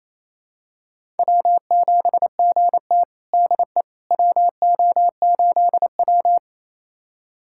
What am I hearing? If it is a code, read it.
Morse code, heard as a keyed tone, about 28 wpm, 700 Hz. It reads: W7GT DE WO8W
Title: USA Callsign DE Callsign - 4 Characters - 28wpm